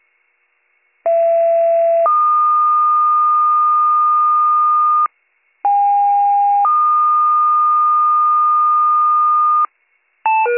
Two-tone page, heard as a one-second tone, then a three-second tone, about 670 then 1220 Hz, about 810 then 1220 Hz. It is heard from 1.1 to 9.7 s.